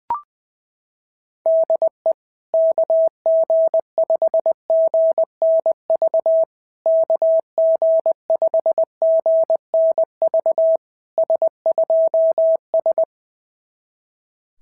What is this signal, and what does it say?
Morse code, heard as a keyed tone, about 20 wpm, 660 Hz. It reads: DE KG5GNV KG5GNV S2S
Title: SOTA QSO Practice 20wpm